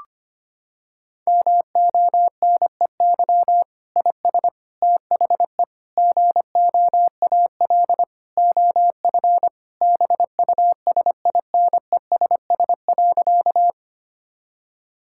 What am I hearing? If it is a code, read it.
Morse code, heard as a keyed tone, about 25 wpm, 700 Hz. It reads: MONEY IS THE GOAL OF BUSINESS.
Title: Sentences from Top 1000 Words - Review Part 5 - 25wpm